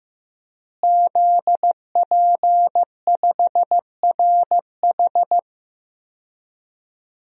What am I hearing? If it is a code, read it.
Morse code, heard as a keyed tone, about 15 wpm, 700 Hz. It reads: ZP5RH
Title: DXCC-100 Call Signs - 5 Characters - Repeated 3x - 15wpm